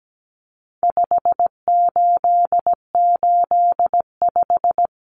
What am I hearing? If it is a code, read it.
Morse code, heard as a keyed tone, about 17 wpm, 700 Hz. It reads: 5885
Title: Numbers - 4-Digit 17wpm